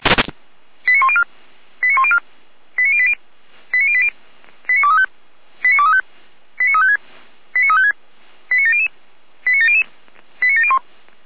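Five-tone selective call (ZVEI), heard as quick runs of five tones, tones of about 70 ms. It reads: 89104 89104 89080 89080 89235 89235 89356 89356 89700 89700 89701
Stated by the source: SDR-received ZVEI five-tone selective call, code 89120.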